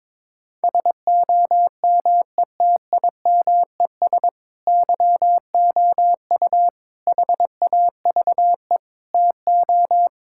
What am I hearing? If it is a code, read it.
Morse code, heard as a keyed tone, about 22 wpm, 700 Hz. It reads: SOMETIMES YOU HAVE TO
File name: practice / 22wpm / Sentences with 3-Word Phrases - 22wpm